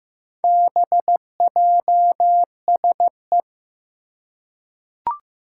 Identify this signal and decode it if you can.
Morse code, heard as a keyed tone, about 15 wpm, 700 Hz. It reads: BJSE
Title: Four Challenging Letter-Number Combinations 15wpm